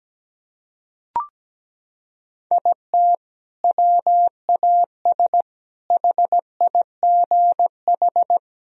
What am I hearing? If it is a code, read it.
Morse code, heard as a keyed tone, about 17 wpm, 700 Hz. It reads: IT WAS HIGH
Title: Sentences from Top 1000 Words - Review Part 1 - 17wpm